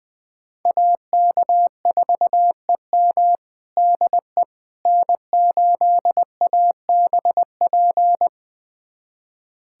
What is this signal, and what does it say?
Morse code, heard as a keyed tone, about 20 wpm, 700 Hz. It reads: AK4EM DE N8ABP
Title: USA Callsign DE Callsign - 5 Characters - 20wpm